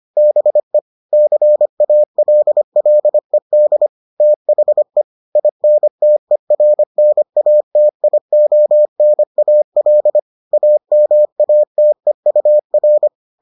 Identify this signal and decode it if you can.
Morse code, heard as a keyed tone, about 25 wpm, 600 Hz. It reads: BE CALLED THE INTERNATIONAL AMATEUR